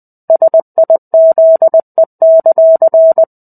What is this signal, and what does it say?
Morse code, heard as a keyed tone, about 20 wpm, 650 Hz. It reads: SIZE;
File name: 05-Chapter-05 20 WPM